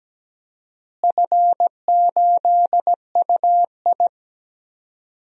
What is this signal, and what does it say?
Morse code, heard as a keyed tone, about 17 wpm, 700 Hz. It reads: F8UI
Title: DXCC-100 Call Signs - 4 Characters - Repeated 2x - 17wpm